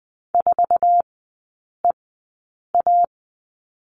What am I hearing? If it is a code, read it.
Morse code, heard as a keyed tone, about 20 wpm, spaced as for 8 wpm, 700 Hz. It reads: 4EA
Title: 047 Lesson 9e - Number 4 - Three Characters 20wpm